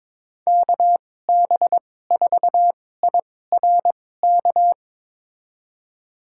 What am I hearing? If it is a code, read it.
Morse code, heard as a keyed tone, about 22 wpm, spaced as for 16 wpm, 700 Hz. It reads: KB4IRK
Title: USA Call Signs - 6 Characters - 2x Character Spacing - 22wpm